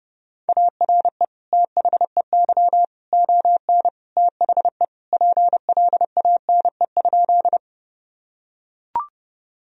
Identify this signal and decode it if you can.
Morse code, heard as a keyed tone, about 30 wpm, 700 Hz. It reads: ARE THEY ON THE PLANE?
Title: Sentences from Top 1000 Words - Review Part 3 - 30wpm